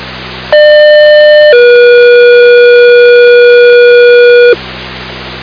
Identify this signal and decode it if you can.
Two-tone page, heard as a one-second tone, then a three-second tone, about 600 then 480 Hz, with noise. It is heard from 0.5 to 4.5 s.